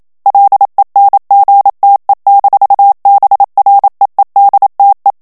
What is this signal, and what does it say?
Morse code, heard as a keyed tone, about 27 wpm, 800 Hz. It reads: LENGTE-BREEDTE
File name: HOLLANDAIS texte clair 1 vitesse 26 mots minute